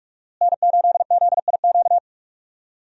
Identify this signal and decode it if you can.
Morse code, heard as a keyed tone, about 41 wpm, 700 Hz. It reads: N8ZIQ
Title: USA Call Signs - 5 Characters - Repeated 2x - 45wpm